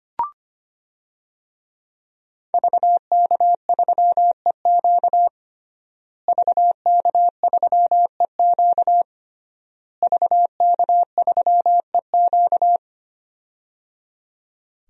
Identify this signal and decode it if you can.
Morse code, heard as a keyed tone, about 25 wpm, 700 Hz. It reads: VK3EQ VK3EQ VK3EQ
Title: DXCC-100 Call Signs - 5 Characters - Repeated 3x - 25wpm